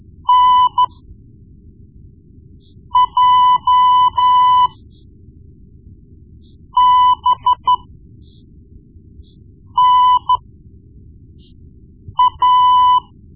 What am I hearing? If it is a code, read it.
Morse code, heard as a keyed tone, about 13 wpm, spaced as for 4 wpm, 1000 Hz. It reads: NJBNA
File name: Morse Lessons / Morse Lessons / Tape-1B